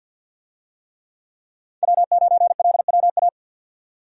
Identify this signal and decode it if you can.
Morse code, heard as a keyed tone, about 50 wpm, 700 Hz. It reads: W9LWA
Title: USA Call Signs - 5 Characters - 50wpm